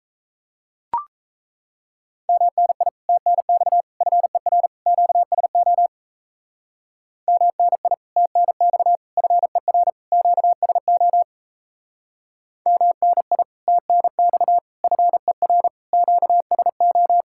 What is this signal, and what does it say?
Morse code, heard as a keyed tone, about 36 wpm, 700 Hz. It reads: MNI TNX FER QSO MNI TNX FER QSO MNI TNX FER QSO